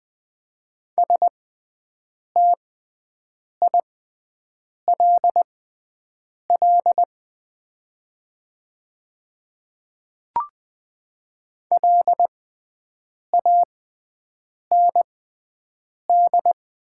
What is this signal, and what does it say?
Morse code, heard as a keyed tone, about 20 wpm, spaced as for 7 wpm, 700 Hz. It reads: STILL LAND
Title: 6x Character Spacing - Top 300 Words - 20wpm